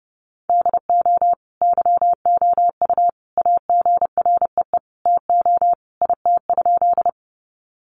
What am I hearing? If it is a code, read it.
Morse code, heard as a keyed tone, about 30 wpm, 700 Hz. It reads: DO YOU AGREE TO IT?
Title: Sentences from Top 1100 Words 30wpm